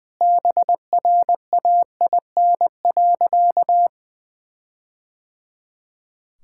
Morse code, heard as a keyed tone, about 20 wpm, 700 Hz. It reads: BRAIN.
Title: Sentences from Top 1600 Words 20wpm